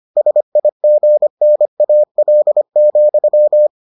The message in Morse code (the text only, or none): SIGNAL,